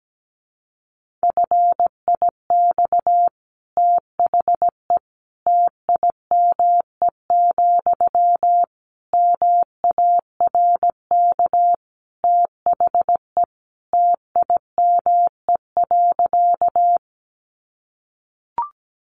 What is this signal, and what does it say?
Morse code, heard as a keyed tone, about 17 wpm, 700 Hz. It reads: FIX THE TIME, MARK THE TIME.
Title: Sentences from Top 1400 Words 17wpm